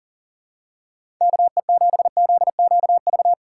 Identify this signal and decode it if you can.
Morse code, heard as a keyed tone, about 37 wpm, 700 Hz. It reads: KE7ZQV
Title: USA Call Signs - 3x Character Spacing - 40wpm